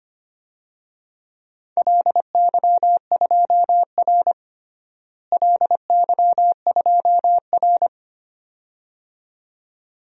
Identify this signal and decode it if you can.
Morse code, heard as a keyed tone, about 25 wpm, 700 Hz. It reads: LY2R LY2R
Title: DXCC-100 Call Signs - 4 Characters - Repeated 2x - 25wpm